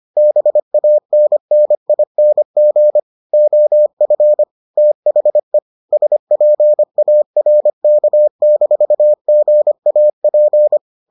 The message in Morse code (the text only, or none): BANNING OF THE SPARK-GAP